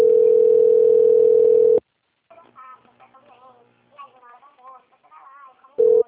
Ringback tone (North America), 440 + 480 Hz.
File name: out- - - - - .38134